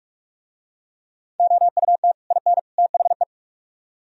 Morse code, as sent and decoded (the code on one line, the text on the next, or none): --- ..- - / .. -. / - .... .
OUT IN THE